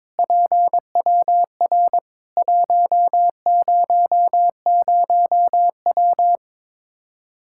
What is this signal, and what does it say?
Morse code, heard as a keyed tone, about 22 wpm, 700 Hz. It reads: PWR 100W
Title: K1USN SST QSO Practice 22wpm